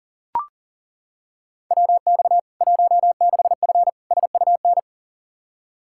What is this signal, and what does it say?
Morse code, heard as a keyed tone, about 37 wpm, 700 Hz. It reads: WX 16F SUN
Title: K1USN SST QSO Practice 40wpm